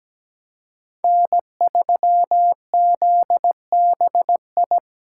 Morse code, heard as a keyed tone, about 20 wpm, 700 Hz. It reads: N3ZBI
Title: Speed Racing - USA 1x3 Call Signs 17wpm